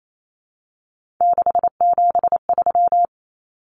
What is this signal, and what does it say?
Morse code, heard as a keyed tone, about 28 wpm, 700 Hz. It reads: 673
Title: Common Cut Numbers - 3-Digit 28wpm